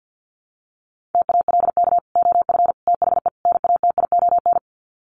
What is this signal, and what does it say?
Morse code, heard as a keyed tone, about 50 wpm, 700 Hz. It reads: TALK OF THE NATION
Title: Sets of 4 Words 50wpm